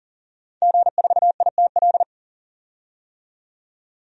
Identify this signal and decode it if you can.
Morse code, heard as a keyed tone, about 40 wpm, 700 Hz. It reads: G4ITL
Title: DXCC-100 Call Signs - 5 Characters - 40wpm